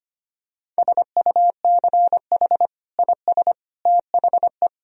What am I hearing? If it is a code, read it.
Morse code, heard as a keyed tone, about 25 wpm, 700 Hz. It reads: SUCH IS THE
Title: Sentences from Top 1000 Words - Review Part 3 - 25wpm